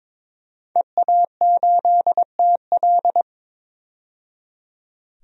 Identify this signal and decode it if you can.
Morse code, heard as a keyed tone, about 22 wpm, 700 Hz. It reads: EA8TL
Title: DXCC-100 Call Signs - 5 Characters - 22wpm